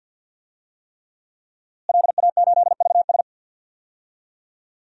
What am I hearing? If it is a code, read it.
Morse code, heard as a keyed tone, about 51 wpm, 700 Hz. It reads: LA8VS
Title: DXCC-100 Call Signs - 5 Characters - Repeated 3x - 50wpm